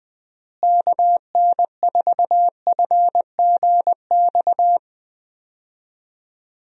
Morse code, heard as a keyed tone, about 20 wpm, 700 Hz. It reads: KN4FGX
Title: USA Call Signs - 6 Characters - Repeated 2x - 20wpm